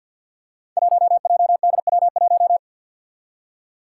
Morse code, heard as a keyed tone, about 53 wpm, 700 Hz. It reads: 1JDW1
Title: Five Challenging Letter-Number Combinations 50wpm